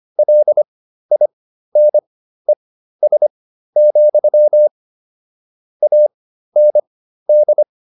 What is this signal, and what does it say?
Morse code, heard as a keyed tone, about 25 wpm, spaced as for 13 wpm, 600 Hz. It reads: LINES, AND